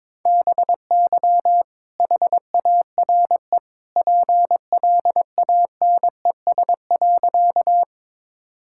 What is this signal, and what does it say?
Morse code, heard as a keyed tone, about 22 wpm, 700 Hz. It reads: BY HARE PLANES.
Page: morse-code-ninja-practice-22wpm